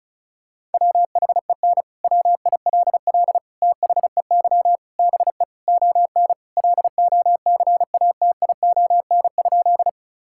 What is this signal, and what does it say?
Morse code, heard as a keyed tone, about 36 wpm, 700 Hz. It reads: WHEN WILL THEY BE ON LOCATION?
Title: Sentences from Top 1000 Words - Review Part 6 - 35wpm